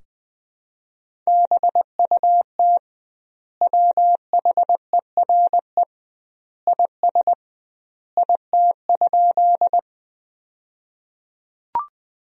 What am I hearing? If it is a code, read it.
Morse code, heard as a keyed tone, about 20 wpm, 700 Hz. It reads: BUT WHERE IS IT?
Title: 2x Word-Spacing - Sentences from Top 200 Words - 20wpm